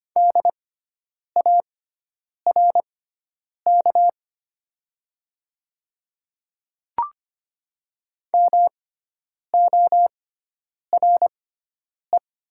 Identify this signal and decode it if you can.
Morse code, heard as a keyed tone, about 25 wpm, spaced as for 9 wpm, 700 Hz. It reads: DARK MORE
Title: 6x Character Spacing - Top 500 Words - 25wpm